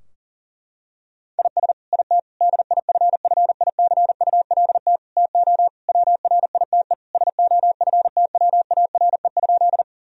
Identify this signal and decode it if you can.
Morse code, heard as a keyed tone, about 40 wpm, 700 Hz. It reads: IS IT DIFFICULT TO WRITE SOFTWARE?
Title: Sentences from Top 1000 Words - Review Part 3 - 40wpm